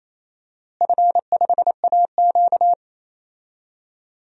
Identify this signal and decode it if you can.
Morse code, heard as a keyed tone, about 27 wpm, 700 Hz. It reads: F5AQ
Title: DXCC-100 Call Signs - 4 Characters - Repeated 3x - 28wpm